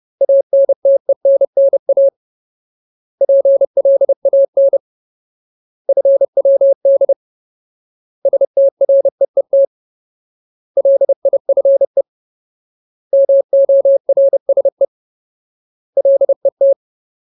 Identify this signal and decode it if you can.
Morse code, heard as a keyed tone, about 30 wpm, 550 Hz. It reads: ANTENNA PLAN FWD STREET LIFE MORSE LET
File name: Ultimate Word List / 30WPM Ultimate